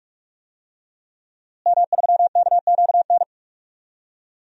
Morse code, heard as a keyed tone, about 46 wpm, 700 Hz. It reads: M3KQN
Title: DXCC-100 Call Signs - 5 Characters - Repeated 3x - 45wpm